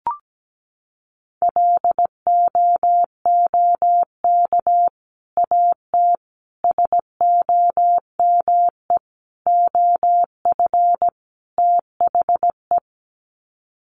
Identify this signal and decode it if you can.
Morse code, heard as a keyed tone, about 17 wpm, 700 Hz. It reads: LOOK AT SOME OF THE
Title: Sets of 5 Words Encore 17wpm